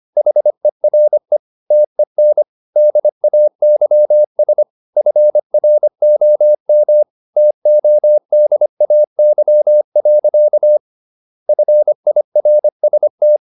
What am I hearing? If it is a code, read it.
Morse code, heard as a keyed tone, about 25 wpm, 600 Hz. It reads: HERE TEN DAYS FROM TODAY. FIRST
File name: Never 0025wpm